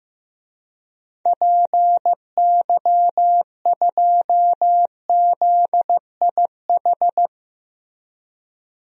Morse code, heard as a keyed tone, about 15 wpm, 700 Hz. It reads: PY2ZIH
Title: DXCC-100 Call Signs - 6 Characters - Repeated 3x - 15wpm